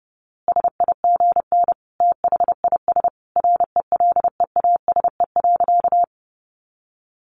Morse code, heard as a keyed tone, about 30 wpm, 700 Hz. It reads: SIGN THIS RELEASE.
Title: A Sound Of Thunder 30wpm